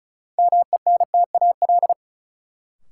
Morse code, heard as a keyed tone, about 34 wpm, 700 Hz. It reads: MENTAL
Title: Sentences from Top 1000 Words - Review Part 4 - New Word Drill - 35wpm